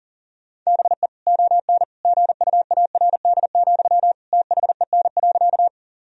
Morse code, heard as a keyed tone, about 37 wpm, 700 Hz. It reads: BE ON GUARD, THEN.
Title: Sentences from Top 1000 Words - Review Part 4 - 40wpm